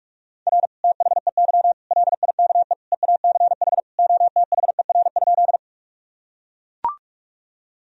Morse code, heard as a keyed tone, about 44 wpm, 700 Hz. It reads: R THEY LIKE EACH OTHER?
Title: Sentences from Top 200 Words and Abbreviations 45wpm